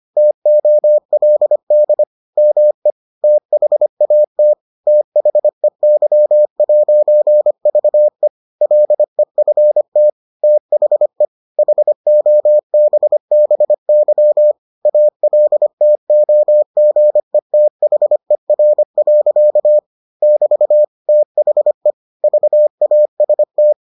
Morse code, heard as a keyed tone, about 25 wpm, 600 Hz. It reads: TOLD ME THAT THEY'VE LEFT THE HOBBY ALTOGETHER. <BT> THE VAST